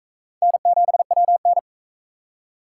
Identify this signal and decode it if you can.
Morse code, heard as a keyed tone, about 40 wpm, 700 Hz. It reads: N7WN